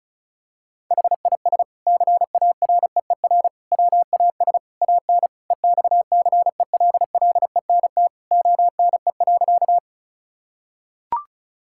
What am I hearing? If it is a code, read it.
Morse code, heard as a keyed tone, about 34 wpm, 690 Hz. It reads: HIS CAREER WAS AN EXCELLENT ONE.
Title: Sentences from Top 1000 Words - Review Part 6 - 35wpm